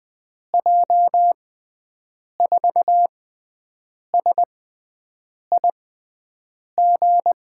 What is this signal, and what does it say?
Morse code, heard as a keyed tone, about 20 wpm, spaced as for 7 wpm, 700 Hz. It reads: J4SIG
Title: USA Call Signs - 6 Characters - 6x Character Spacing - 20wpm